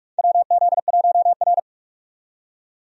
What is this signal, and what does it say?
Morse code, heard as a keyed tone, about 46 wpm, 700 Hz. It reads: WZ1R
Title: USA Call Signs - 4 Characters - 45wpm